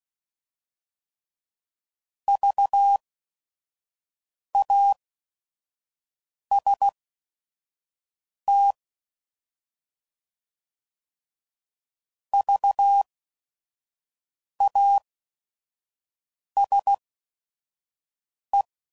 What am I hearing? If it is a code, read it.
Morse code, heard as a keyed tone, about 16 wpm, spaced as for 5 wpm, 800 Hz. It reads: VAST VASE
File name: Prefixes / By First Letter / Vs